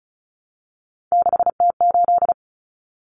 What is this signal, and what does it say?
Morse code, heard as a keyed tone, about 34 wpm, 700 Hz. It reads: 6T8
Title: Common Cut Numbers - 3-Digit 35wpm